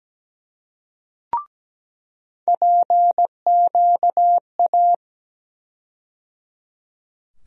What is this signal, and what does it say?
Morse code, heard as a keyed tone, about 17 wpm, 700 Hz. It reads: PQA
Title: USA Call Sign Postfixes - 3 Letters 17wpm